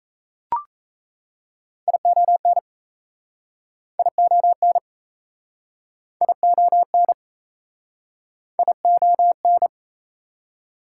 Morse code, heard as a keyed tone, about 38 wpm, 700 Hz. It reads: ION ION ION ION